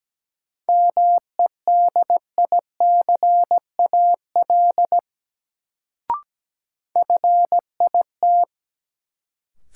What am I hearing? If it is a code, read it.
Morse code, heard as a keyed tone, about 17 wpm, 700 Hz. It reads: MEDICAL FIT
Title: Sentences from Top 800 Words - New Word Drill - 17wpm